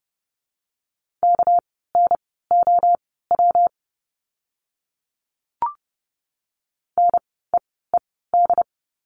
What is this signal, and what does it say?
Morse code, heard as a keyed tone, about 30 wpm, spaced as for 17 wpm, 700 Hz. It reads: KNOW NEED
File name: course / 30:10wpm / 230 Lesson 37f - Letter x - Words 30wpm